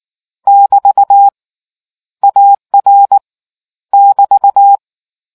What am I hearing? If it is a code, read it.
Morse code, heard as a keyed tone, about 19 wpm, 800 Hz. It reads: <BT> AR <BT>